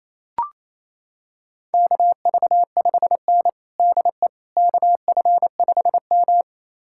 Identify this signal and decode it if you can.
Morse code, heard as a keyed tone, about 28 wpm, 700 Hz. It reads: KV5N DE KF5M
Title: USA Callsign DE Callsign - 4 Characters - 28wpm